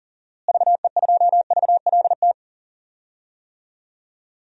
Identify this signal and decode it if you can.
Morse code, heard as a keyed tone, about 37 wpm, 700 Hz. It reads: VE2VLT
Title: DXCC-100 Call Signs - 6 Characters - Repeated 2x - 40wpm